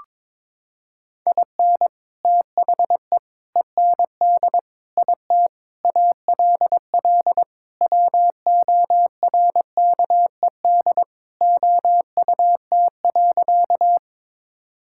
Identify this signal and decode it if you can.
Morse code, heard as a keyed tone, about 22 wpm, 700 Hz. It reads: IN THE END IT ALL WORKED OUT.
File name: practice / 22wpm / Sentences with 3-Word Phrases - 22wpm